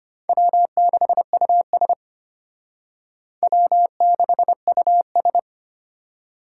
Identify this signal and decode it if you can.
Morse code, heard as a keyed tone, about 32 wpm, 700 Hz. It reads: W6US W6US